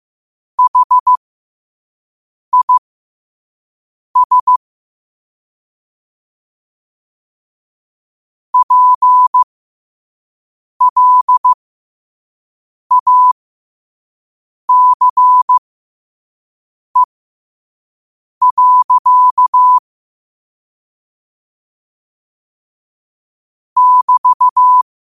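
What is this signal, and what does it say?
Morse code, heard as a keyed tone, about 15 wpm, spaced as for 5 wpm, 1000 Hz. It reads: HIS PLACE. <BT>
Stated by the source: Morse practice quote sent at 5 wpm.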